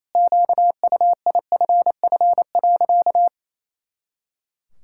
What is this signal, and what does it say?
Morse code, heard as a keyed tone, about 27 wpm, 700 Hz. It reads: QUIFF.
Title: Pangram Challenge v2 - 28wpm